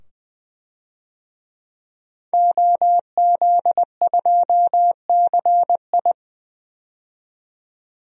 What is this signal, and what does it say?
Morse code, heard as a keyed tone, about 20 wpm, 700 Hz. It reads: OZ2CI
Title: DXCC-100 Call Signs - 5 Characters - 20wpm